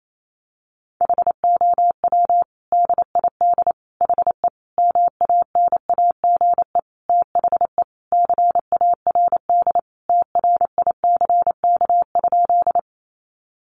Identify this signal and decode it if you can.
Morse code, heard as a keyed tone, about 28 wpm, 700 Hz. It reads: HOW DID HE MANAGE THE CARD TRICK?
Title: Sentences from Top 1500 Words 28wpm